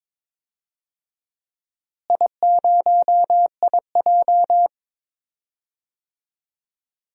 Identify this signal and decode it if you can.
Morse code, heard as a keyed tone, about 22 wpm, 700 Hz. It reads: I0IJ
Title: DXCC-100 Call Signs - 4 Characters - Repeated 2x - 22wpm